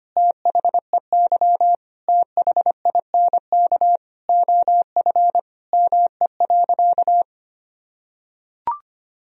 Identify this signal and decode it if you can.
Morse code, heard as a keyed tone, about 25 wpm, 700 Hz. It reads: THEY THINK OF ME.
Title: Sentences from Top 1000 Words - Review Part 1 - 25wpm